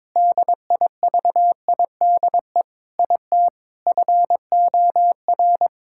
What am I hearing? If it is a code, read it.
Morse code, heard as a keyed tone, about 22 wpm, 700 Hz. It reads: DIVIDE IT FOR
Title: Sentences from Top 1600 Words 22wpm